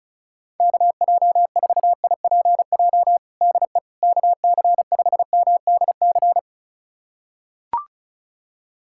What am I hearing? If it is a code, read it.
Morse code, heard as a keyed tone, about 34 wpm, 700 Hz. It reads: KJ4IPJ DE KC5MDC
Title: USA Callsign DE Callsign - 6 Characters - 35wpm